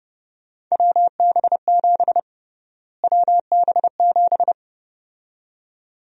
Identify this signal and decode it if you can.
Morse code, heard as a keyed tone, about 32 wpm, 700 Hz. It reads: WB7 WB7
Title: USA Call Sign Prefixes - Repeated 2x - 30wpm